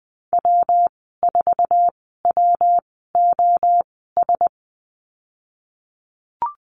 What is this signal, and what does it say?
Morse code, heard as a keyed tone, about 19 wpm, spaced as for 14 wpm, 700 Hz. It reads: W4WOS
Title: 122 Lesson 20g - Number 3 - Call Signs 20wpm